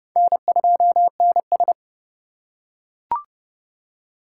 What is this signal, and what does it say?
Morse code, heard as a keyed tone, about 31 wpm, 700 Hz. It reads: N2NS
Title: K1USN SST QSO Practice - 30wpm